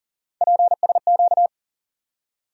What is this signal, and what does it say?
Morse code, heard as a keyed tone, about 40 wpm, 700 Hz. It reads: PSQ
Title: USA Call Sign Postfixes - 3 Letters 40wpm